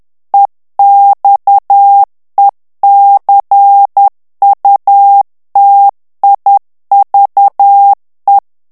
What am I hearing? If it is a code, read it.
Morse code, heard as a keyed tone, about 10 wpm, 800 Hz. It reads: EXECUTIVE
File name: ANGLAIS texte clair 3 vitesse 10 mots minute